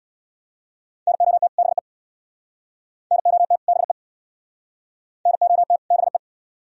Morse code, heard as a keyed tone, about 63 wpm, 700 Hz. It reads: NOT BE NOT BE NOT BE